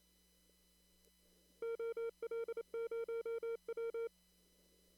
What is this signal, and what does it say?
Morse code, heard as a keyed tone, about 27 wpm, 1380 Hz, with noise. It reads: OL0W
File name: CW / kt5j wpx cw-op-k5pi